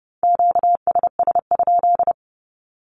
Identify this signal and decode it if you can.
Morse code, heard as a keyed tone, about 30 wpm, 700 Hz. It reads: QSS?
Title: Q-Codes Repeated 3x 30wpm